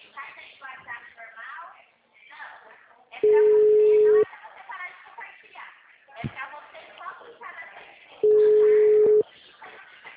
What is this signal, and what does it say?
Ringback tone (Europe, 425 Hz), with speech.